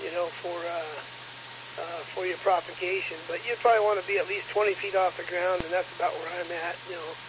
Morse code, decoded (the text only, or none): none